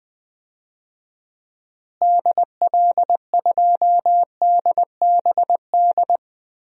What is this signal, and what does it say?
Morse code, heard as a keyed tone, about 20 wpm, 700 Hz. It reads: DL2DBD
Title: DXCC-100 Call Signs - 6 Characters - Repeated 2x - 20wpm